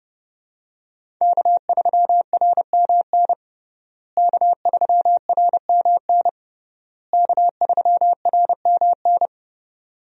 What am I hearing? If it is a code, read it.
Morse code, heard as a keyed tone, about 32 wpm, 700 Hz. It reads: K3RMN K3RMN K3RMN
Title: USA Call Signs - 5 Characters - Repeated 3x - 30wpm